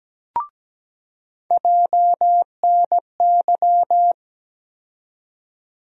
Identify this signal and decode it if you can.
Morse code, heard as a keyed tone, about 17 wpm, 700 Hz. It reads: JNY
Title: USA Call Sign Postfixes - 3 Letters 17wpm